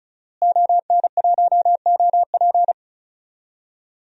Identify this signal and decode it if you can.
Morse code, heard as a keyed tone, about 36 wpm, 700 Hz. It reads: ON1OP
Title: DXCC-100 Call Signs - 5 Characters - Repeated 2x - 35wpm